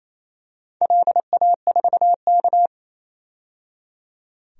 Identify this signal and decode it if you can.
Morse code, heard as a keyed tone, about 28 wpm, 700 Hz. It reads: LA4K
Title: DXCC-100 Call Signs - 4 Characters - 28wpm